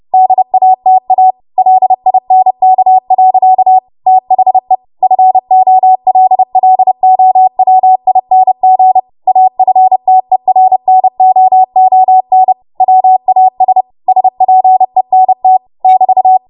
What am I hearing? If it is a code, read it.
Morse code, heard as a keyed tone, about 30 wpm, 750 Hz. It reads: DATA LINK. THE FOLLOWING AFTERNOON WAS SPENT <BT>
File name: Morse / 30WPM